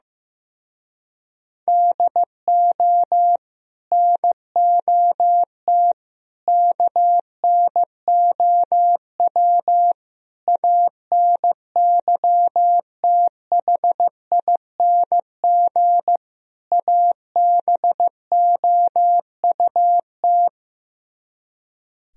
Morse code, heard as a keyed tone, about 15 wpm, 700 Hz. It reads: DO NOT KNOW ANYTHING ABOUT
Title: Speed-Racing - Sets of 5 Words 15wpm